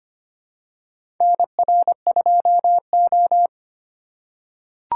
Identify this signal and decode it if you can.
Morse code, heard as a keyed tone, about 25 wpm, 700 Hz. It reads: NR2O
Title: USA Call Signs - 3x Character Spacing - 25wpm